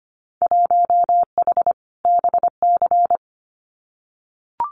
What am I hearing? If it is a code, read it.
Morse code, heard as a keyed tone, about 25 wpm, 690 Hz. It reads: 1H BC
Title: Winter Field Day Exchanges 25wpm